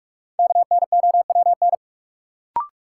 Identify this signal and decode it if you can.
Morse code, heard as a keyed tone, about 43 wpm, 700 Hz. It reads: KNOWN
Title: Sentences from Top 600 Words - New Word Drill - 45wpm